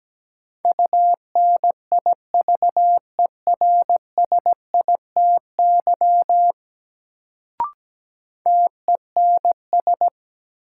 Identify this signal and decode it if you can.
Morse code, heard as a keyed tone, about 17 wpm, 700 Hz. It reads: UNIVERSITY TENS
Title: Sets of 4 Words - New Word Drill - 17wpm